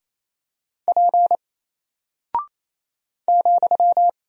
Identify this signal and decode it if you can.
Morse code, heard as a keyed tone, about 28 wpm, 700 Hz. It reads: P ,
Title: Single Character (Letter:Number:Punctuation:Prosign) Alt 28wpm